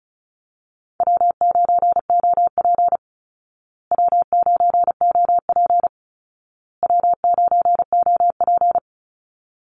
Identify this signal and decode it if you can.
Morse code, heard as a keyed tone, about 34 wpm, 700 Hz. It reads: W9OP W9OP W9OP
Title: Call Signs Repeated 2x - 35wpm